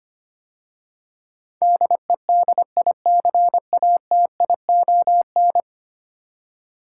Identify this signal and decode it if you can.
Morse code, heard as a keyed tone, about 25 wpm, 700 Hz. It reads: DEDICATION
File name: 10 Letter Words - First 3 Letters Given - 25wpm